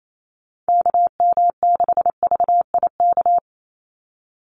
Morse code, heard as a keyed tone, about 28 wpm, 700 Hz. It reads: KM6VIK
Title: USA Call Signs 28wpm